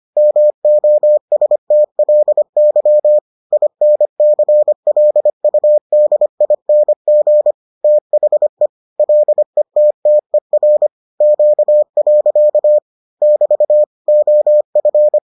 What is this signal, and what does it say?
Morse code, heard as a keyed tone, about 25 wpm, 600 Hz. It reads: MOSTLY INCLUDING THE LETTER Q. <BT> OF